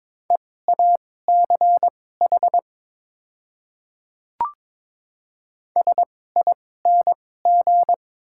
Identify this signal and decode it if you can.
Morse code, heard as a keyed tone, about 22 wpm, spaced as for 16 wpm, 700 Hz. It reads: EACH SING
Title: 2x Character Spacing - Top 500 Words - 22wpm